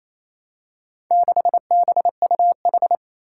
Morse code, heard as a keyed tone, about 27 wpm, 700 Hz. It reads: 6BUH
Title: Four Challenging Letter-Number Combinations 28wpm